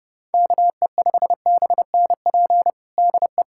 Morse code, heard as a keyed tone, about 32 wpm, 700 Hz. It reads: KE5BNP DE
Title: USA Callsign DE Callsign 30wpm